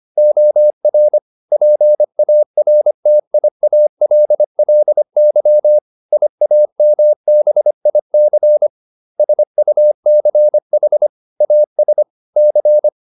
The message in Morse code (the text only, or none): OR PARTIALLY IAMBIC SUCH AS C